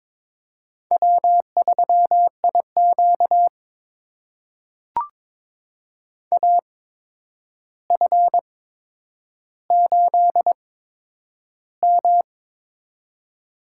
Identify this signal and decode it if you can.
Morse code, heard as a keyed tone, about 23 wpm, spaced as for 10 wpm, 700 Hz. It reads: W3IQ AF8M
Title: USA Call Signs - 8x Character Spacing - 22wpm